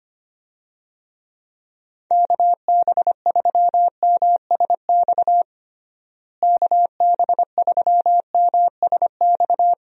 Morse code, heard as a keyed tone, about 25 wpm, 700 Hz. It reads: KB3MSX KB3MSX
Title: USA Call Signs - 6 Characters - Repeated 3x - 25wpm